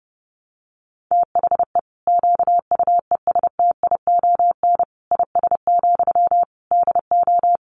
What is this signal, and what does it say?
Morse code, heard as a keyed tone, about 30 wpm, 700 Hz. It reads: THE QUESTION IS, DO
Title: Sentences from Top 1000 Words 30wpm